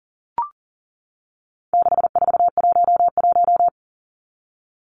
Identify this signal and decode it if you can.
Morse code, heard as a keyed tone, about 40 wpm, 700 Hz. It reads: 6411
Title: Numbers - 4-Digit 40wpm